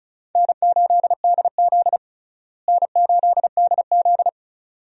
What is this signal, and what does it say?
Morse code, heard as a keyed tone, about 36 wpm, 700 Hz. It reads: N8DZ N8DZ
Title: USA Call Signs - 4 Characters - Repeated 2x - 35wpm